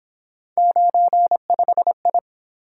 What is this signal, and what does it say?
Morse code, heard as a keyed tone, about 27 wpm, 700 Hz. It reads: 95I
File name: Speed-Racing - Three Letter-Number Combinations - 22wpm